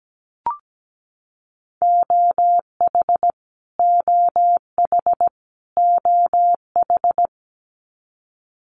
Morse code, heard as a keyed tone, about 17 wpm, 700 Hz. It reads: OH OH OH
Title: Top 500 Words Repeated 17wpm